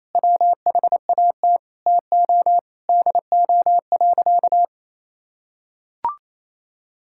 Morse code, heard as a keyed tone, about 28 wpm, 700 Hz. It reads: WHAT TO DO.
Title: Sentences from Top 1000 Words - Review Part 6 - 28wpm